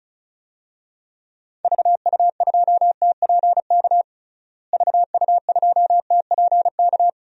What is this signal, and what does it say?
Morse code, heard as a keyed tone, about 34 wpm, 700 Hz. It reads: VU2TPK VU2TPK
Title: DXCC-100 Call Signs - 6 Characters - Repeated 3x - 35wpm